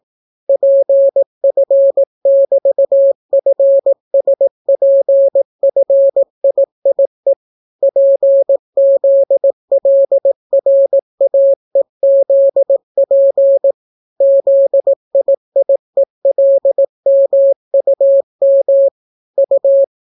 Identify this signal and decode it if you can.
Morse code, heard as a keyed tone, about 18 wpm, 550 Hz. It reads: PF<BT>FSPFIIE PZLRAEZP ZIIELMUM U